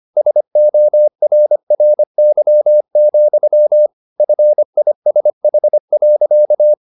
Morse code, heard as a keyed tone, about 25 wpm, 600 Hz. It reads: SORRY, FISH.